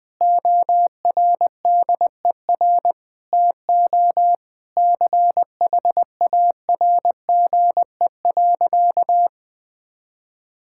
Morse code, heard as a keyed tone, about 20 wpm, 700 Hz. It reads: ORDER TO CHARGE.
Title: Sentences from Top 1000 Words - Review Part 6 - 20wpm